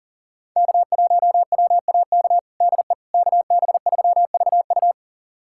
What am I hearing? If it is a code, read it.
Morse code, heard as a keyed tone, about 37 wpm, 700 Hz. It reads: K1WAK DE KB3VU
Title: USA Callsign DE Callsign - 5 Characters - 40wpm